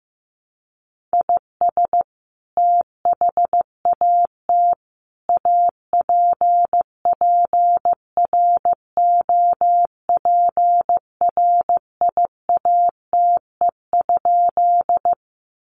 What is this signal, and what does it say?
Morse code, heard as a keyed tone, about 15 wpm, 700 Hz. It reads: IS THAT APPROPRIATE?
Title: Sentences from Top 900 Words 15wpm